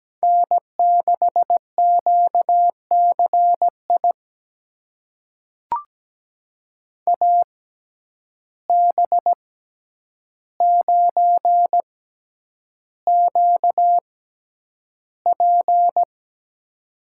Morse code, heard as a keyed tone, about 17 wpm, spaced as for 9 wpm, 700 Hz. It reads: N6QCI AB9QP
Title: USA Call Signs - 5 Characters - 6x Character Spacing - 17wpm